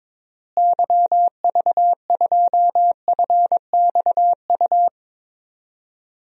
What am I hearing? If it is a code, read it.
Morse code, heard as a keyed tone, about 22 wpm, 700 Hz. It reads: YV2FXU
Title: DXCC-100 Call Signs - 6 Characters - Repeated 2x - 22wpm